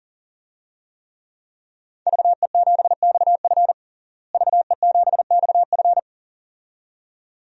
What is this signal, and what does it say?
Morse code, heard as a keyed tone, about 40 wpm, 700 Hz. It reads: VE7XF VE7XF
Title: DXCC-100 Call Signs - 5 Characters - Repeated 2x - 40wpm